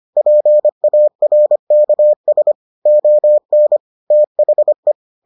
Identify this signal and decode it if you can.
Morse code, heard as a keyed tone, about 25 wpm, 600 Hz. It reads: PARKS ON THE